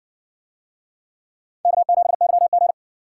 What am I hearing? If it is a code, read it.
Morse code, heard as a keyed tone, about 60 wpm, 700 Hz. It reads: K7YG